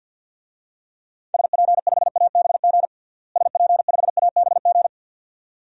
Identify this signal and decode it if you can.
Morse code, heard as a keyed tone, about 52 wpm, 700 Hz. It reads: SP5ABG SP5ABG
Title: DXCC-100 Call Signs - 6 Characters - Repeated 2x - 50wpm